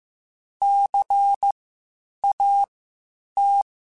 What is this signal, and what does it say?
Morse code, heard as a keyed tone, about 15 wpm, spaced as for 8 wpm, 800 Hz. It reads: CAT
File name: Spaced Alphas / Spaced Alphas mp3s / Cs sp @13